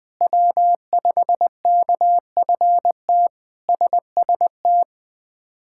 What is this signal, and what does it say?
Morse code, heard as a keyed tone, about 20 wpm, 700 Hz. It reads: W5KFT SST
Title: K1USN SST QSO Practice - 20wpm